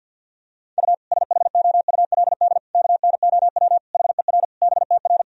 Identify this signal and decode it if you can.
Morse code, heard as a keyed tone, about 52 wpm, 700 Hz. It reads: U SHOULD KNOW HER BTR
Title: Sentences from Top 400 Words and Abbreviations 50wpm